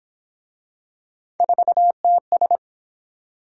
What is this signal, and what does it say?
Morse code, heard as a keyed tone, about 31 wpm, 700 Hz. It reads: 4TS